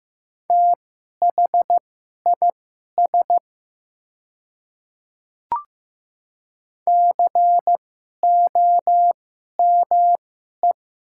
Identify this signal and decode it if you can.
Morse code, heard as a keyed tone, about 15 wpm, spaced as for 11 wpm, 700 Hz. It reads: THIS COME
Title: 2x Character Spacing - Top 100 Words - 15wpm